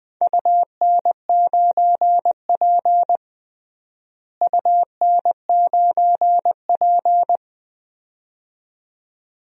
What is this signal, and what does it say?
Morse code, heard as a keyed tone, about 20 wpm, 700 Hz. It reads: UN9P UN9P
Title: DXCC-100 Call Signs - 4 Characters - Repeated 2x - 20wpm